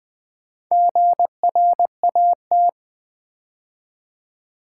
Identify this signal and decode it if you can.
Morse code, heard as a keyed tone, about 20 wpm, 700 Hz. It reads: GRAT
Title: Build A Word - 2 Letters At A Time - 8-Letter Words - 20wpm